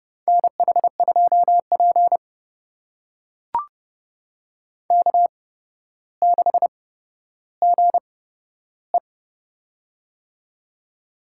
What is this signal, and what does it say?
Morse code, heard as a keyed tone, about 30 wpm, spaced as for 13 wpm, 700 Hz. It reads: NH2P K6GE
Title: USA Call Signs - 4 Characters - 8x Character Spacing - 30wpm